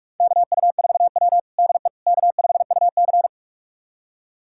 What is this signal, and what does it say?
Morse code, heard as a keyed tone, about 46 wpm, 700 Hz. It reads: KU4W DE K5UC
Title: USA Callsign DE Callsign - 4 Characters - 45wpm